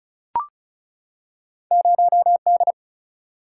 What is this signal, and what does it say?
Morse code, heard as a keyed tone, about 36 wpm, 700 Hz. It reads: 0 D
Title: Single-Letter Single-Number 35wpm